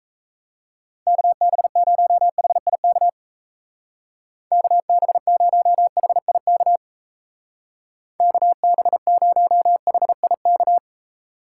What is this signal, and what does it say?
Morse code, heard as a keyed tone, about 35 wpm, 700 Hz. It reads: KB0HIK KB0HIK KB0HIK
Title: Speed Racing - USA 2x3 Call Signs 28wpm